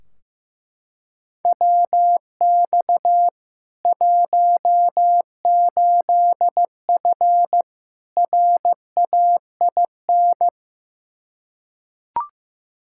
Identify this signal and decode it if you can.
Morse code, heard as a keyed tone, about 15 wpm, 700 Hz. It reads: WX 18F RAIN
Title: Practice QSO - Exchange 2 - Info - 15wpm